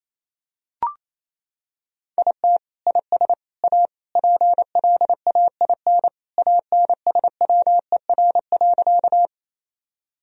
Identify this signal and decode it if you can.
Morse code, heard as a keyed tone, about 28 wpm, 700 Hz. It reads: IT IS A PLAIN ANSWER.
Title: Sentences from Top 1000 Words - Review Part 2 - 28wpm